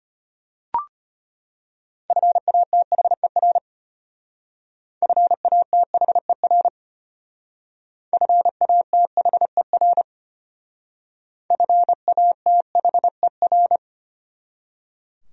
Morse code, heard as a keyed tone, about 32 wpm, 700 Hz. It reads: FATHER FATHER FATHER FATHER